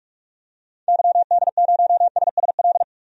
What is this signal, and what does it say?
Morse code, heard as a keyed tone, about 46 wpm, 700 Hz. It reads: YD0SSL
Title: DXCC-100 Call Signs - 6 Characters - Repeated 2x - 45wpm